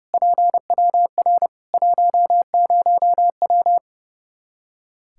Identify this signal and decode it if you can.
Morse code, heard as a keyed tone, about 31 wpm, 700 Hz. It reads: PWR 10W
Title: Practice QSO - Exchange 2 - Info - 30wpm